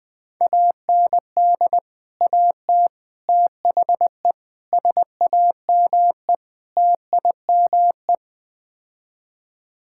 Morse code, heard as a keyed tone, about 20 wpm, 700 Hz. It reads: AND AT THE SAME TIME
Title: Build A Phrase - 5-Word Phrases - 20wpm